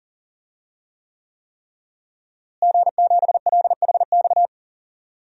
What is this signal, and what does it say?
Morse code, heard as a keyed tone, about 39 wpm, 700 Hz. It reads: G7LHX
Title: DXCC-100 Call Signs - 5 Characters - Repeated 3x - 40wpm